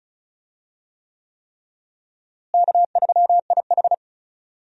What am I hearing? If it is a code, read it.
Morse code, heard as a keyed tone, about 34 wpm, 700 Hz. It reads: K3IH
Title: USA Call Signs - 4 Characters - Repeated 3x - 35wpm